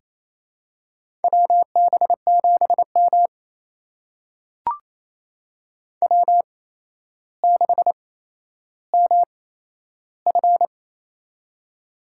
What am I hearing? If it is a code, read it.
Morse code, heard as a keyed tone, about 28 wpm, spaced as for 12 wpm, 700 Hz. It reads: WB7M W6MF
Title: USA Call Signs - 4 Characters - 8x Character Spacing - 28wpm